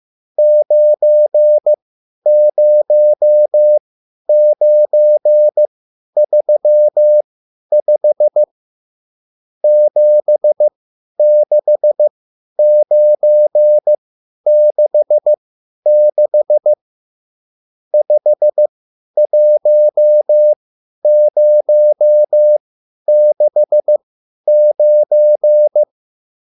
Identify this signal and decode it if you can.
Morse code, heard as a keyed tone, about 15 wpm, spaced as for 10 wpm, 600 Hz. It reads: 90935 76966 51069
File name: QSL / CW lesson